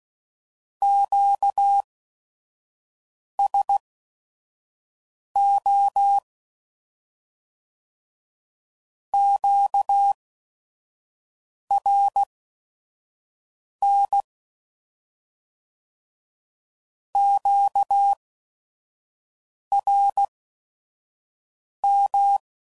Morse code, heard as a keyed tone, about 16 wpm, spaced as for 5 wpm, 800 Hz. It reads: QSO QRN QRM
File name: CW Code Groups @5WPM / Q Calls / Q Calls Once